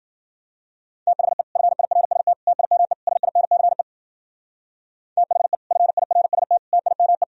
Morse code, heard as a keyed tone, about 59 wpm, 700 Hz. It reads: THE FIRST TIME SINCE THE FIRST TIME